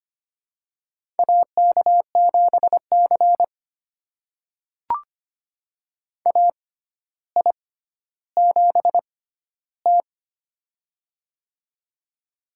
Morse code, heard as a keyed tone, about 25 wpm, spaced as for 13 wpm, 700 Hz. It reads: AK7C AI7T
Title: USA Call Signs - 4 Characters - 6x Character Spacing - 25wpm